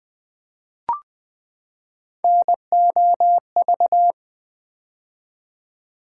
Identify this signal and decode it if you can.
Morse code, heard as a keyed tone, about 20 wpm, 700 Hz. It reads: NOV
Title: Build A Word - 3 Letters At A Time - 8-Letter Words - 20wpm